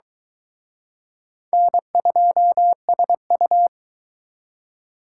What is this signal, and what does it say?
Morse code, heard as a keyed tone, about 23 wpm, 700 Hz. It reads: N2SU